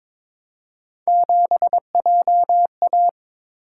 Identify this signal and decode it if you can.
Morse code, heard as a keyed tone, about 26 wpm, 700 Hz. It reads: 7JA
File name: Speed-Racing - Three Letter-Number Combinations - 22wpm